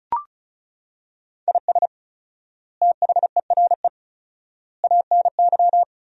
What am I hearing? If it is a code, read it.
Morse code, heard as a keyed tone, about 34 wpm, 700 Hz. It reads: IS THERE ANY
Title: 4x Word-Spacing - Sentences from Top 300 Words - 35wpm